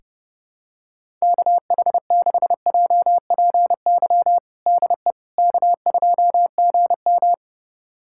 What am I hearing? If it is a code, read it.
Morse code, heard as a keyed tone, about 30 wpm, 700 Hz. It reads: KH6JPY DE K2GM
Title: USA Callsign DE Callsign 30wpm